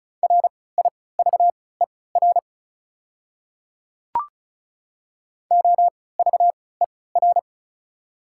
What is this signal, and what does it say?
Morse code, heard as a keyed tone, about 36 wpm, spaced as for 20 wpm, 700 Hz. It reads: RIVER OVER
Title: 3x Character Spacing - Top 500 Words - 35wpm